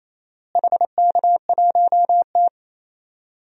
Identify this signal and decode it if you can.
Morse code, heard as a keyed tone, about 28 wpm, 700 Hz. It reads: HK1T
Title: DXCC-100 Call Signs - 4 Characters - Repeated 3x - 28wpm